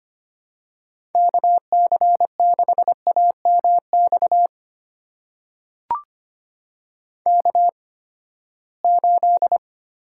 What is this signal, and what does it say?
Morse code, heard as a keyed tone, about 25 wpm, spaced as for 17 wpm, 700 Hz. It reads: KC6AMX K8
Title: USA Call Signs - 8x Character Spacing - 25wpm